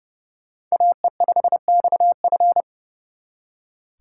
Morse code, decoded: AE5XF

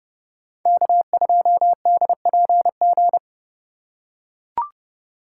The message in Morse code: K2DPG

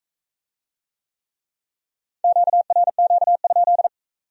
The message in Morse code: QRQ?

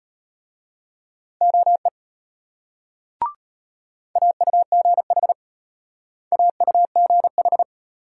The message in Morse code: OE AUGH AUGH

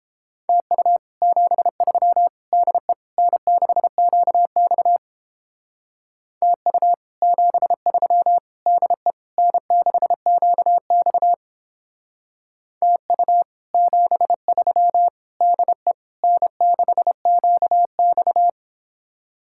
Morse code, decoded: TU 73 DE N6QX TU 73 DE N6QX TU 73 DE N6QX

